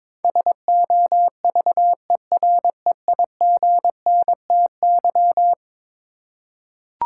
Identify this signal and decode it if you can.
Morse code, heard as a keyed tone, about 22 wpm, 700 Hz. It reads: SOVEREIGNTY
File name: practice / 22wpm / Eleven Letter Words - 22wpm